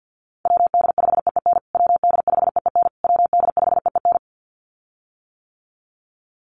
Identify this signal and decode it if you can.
Morse code, heard as a keyed tone, about 50 wpm, 700 Hz. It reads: WD5EEN WD5EEN WD5EEN
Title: USA Call Signs Repeated 3x 50wpm